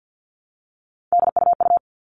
Morse code, heard as a keyed tone, about 50 wpm, 700 Hz. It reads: DUU
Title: All Cut Numbers - 3-Digit 50wpm